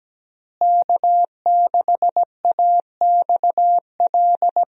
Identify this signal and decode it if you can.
Morse code, heard as a keyed tone, about 17 wpm, 700 Hz. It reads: K6AXL
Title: USA Call Signs - 5 Characters - Repeated 3x - 17wpm